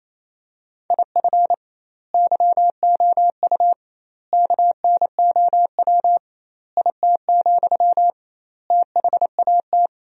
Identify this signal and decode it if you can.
Morse code, heard as a keyed tone, about 28 wpm, 700 Hz. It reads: IF YOU KNOW IT, THAT
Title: 2x Word-Spacing - Sentences from Top 400 Words - 28wpm